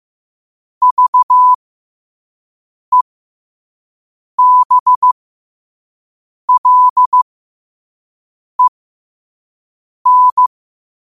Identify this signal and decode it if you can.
Morse code, heard as a keyed tone, about 15 wpm, spaced as for 5 wpm, 1000 Hz. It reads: VEBLEN
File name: qotdfiles / 5wpm